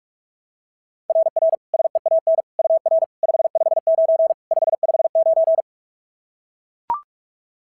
Morse code, heard as a keyed tone, about 44 wpm, 690 Hz. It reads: RR SEAN UR 559 559
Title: SOTA QSO Practice 45wpm